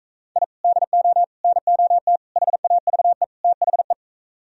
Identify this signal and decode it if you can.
Morse code, heard as a keyed tone, about 42 wpm, 700 Hz. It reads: I DO NOT HAVE THE